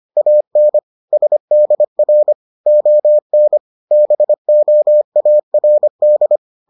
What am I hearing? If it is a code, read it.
Morse code, heard as a keyed tone, about 25 wpm, 600 Hz. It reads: AN SDR ON BOARD